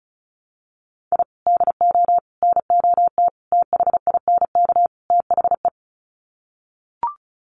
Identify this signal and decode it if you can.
Morse code, heard as a keyed tone, about 34 wpm, 700 Hz. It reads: I DO NOT THINK THE
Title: Sets of 5 Words 35wpm